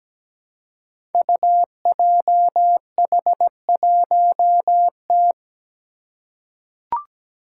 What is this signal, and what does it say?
Morse code, heard as a keyed tone, about 17 wpm, 700 Hz. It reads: UJH1T
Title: Five Challenging Letter-Number Combinations 17wpm